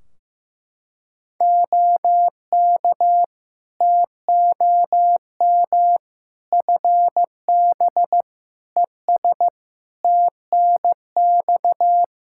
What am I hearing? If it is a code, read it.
Morse code, heard as a keyed tone, about 16 wpm, 700 Hz. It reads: OK TOM FB ES TNX
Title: Practice QSO - Exchange 2 - Info - 15wpm